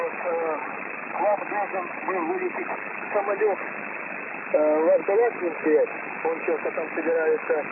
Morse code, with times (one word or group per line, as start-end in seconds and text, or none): none